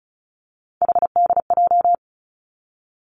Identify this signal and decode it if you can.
Morse code, heard as a keyed tone, about 36 wpm, 700 Hz. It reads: HDJ
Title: Three Challenging Letter-Number Combinations 35wpm